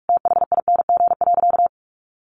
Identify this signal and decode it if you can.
Morse code, heard as a keyed tone, about 44 wpm, 700 Hz. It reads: THING.
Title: Sentences from Top 400 Words 45wpm